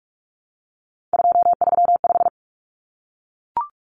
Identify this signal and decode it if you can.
Morse code, heard as a keyed tone, about 44 wpm, 700 Hz. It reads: 235
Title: Common Cut Numbers - 3-Digit 45wpm